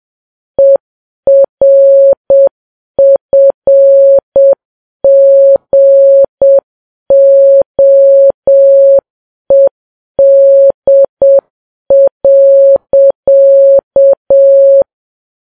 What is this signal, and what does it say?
Morse code, heard as a keyed tone, about 7 wpm, 550 Hz. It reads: ERFGOED.
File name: Files Web tls / Morse on9fv / morse7kps